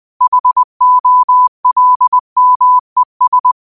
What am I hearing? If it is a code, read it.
Morse code, heard as a keyed tone, about 20 wpm, 1000 Hz. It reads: HOLMES
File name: qotdfiles / 20wpm